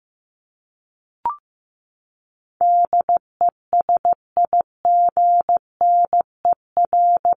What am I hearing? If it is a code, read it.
Morse code, heard as a keyed tone, about 15 wpm, 700 Hz. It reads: DESIGNER
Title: Top 1501-1750 Words 15wpm